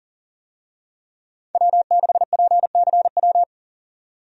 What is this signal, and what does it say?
Morse code, heard as a keyed tone, about 38 wpm, 700 Hz. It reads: W6PCW
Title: USA Call Signs - 5 Characters - 40wpm